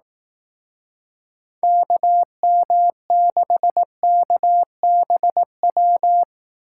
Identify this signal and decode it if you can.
Morse code, heard as a keyed tone, about 18 wpm, 700 Hz. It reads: KM6KBW
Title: Speed Racing - USA Call Signs 15wpm